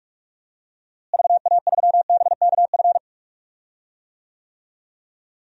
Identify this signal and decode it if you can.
Morse code, heard as a keyed tone, about 46 wpm, 700 Hz. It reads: VA3BKF
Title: DXCC-100 Call Signs - 6 Characters - Repeated 3x - 45wpm